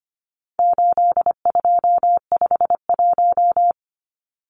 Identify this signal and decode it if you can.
Morse code, heard as a keyed tone, about 25 wpm, 700 Hz. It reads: 8251
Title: Numbers - 4-Digit 25wpm